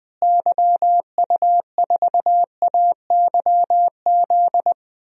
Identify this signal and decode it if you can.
Morse code, heard as a keyed tone, about 20 wpm, 700 Hz. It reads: YU4AYZ
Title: DXCC-100 Call Signs - 6 Characters - Repeated 3x - 20wpm